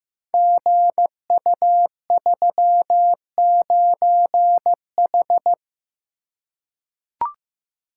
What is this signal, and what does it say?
Morse code, heard as a keyed tone, about 15 wpm, 700 Hz. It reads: GU39H
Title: Code Groups - 5 Letter:Numbers - 15wpm